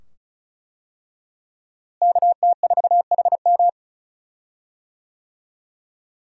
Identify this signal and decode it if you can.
Morse code, heard as a keyed tone, about 36 wpm, 700 Hz. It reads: KT4HM
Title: USA Call Signs - 5 Characters - Repeated 2x - 35wpm